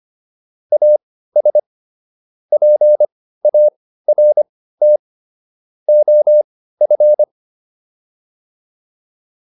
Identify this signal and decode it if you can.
Morse code, heard as a keyed tone, about 25 wpm, spaced as for 15 wpm, 600 Hz. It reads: AS PART OF
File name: WD204 15